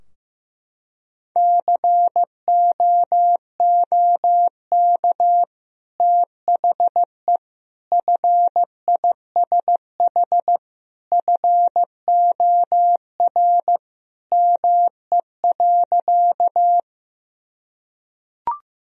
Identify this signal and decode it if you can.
Morse code, heard as a keyed tone, about 15 wpm, 700 Hz. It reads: COOK THE FISH FOR ME.
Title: Sentences from Top 1000 Words - Review Part 4 - 15wpm